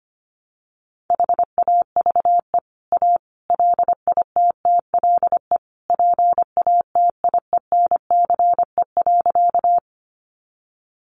Morse code, heard as a keyed tone, about 25 wpm, 700 Hz. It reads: HAVE A LITTLE PATIENCE.
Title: Sentences from Top 1300 Words 25wpm